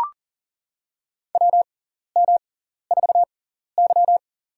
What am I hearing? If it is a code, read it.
Morse code, heard as a keyed tone, about 37 wpm, spaced as for 14 wpm, 700 Hz. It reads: WM4Y
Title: USA Call Signs - 4 Characters - 6x Character Spacing - 40wpm